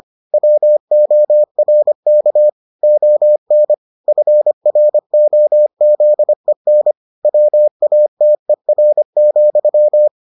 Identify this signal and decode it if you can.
Morse code, heard as a keyed tone, about 25 wpm, 600 Hz. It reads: WORK ON FROZEN WATER,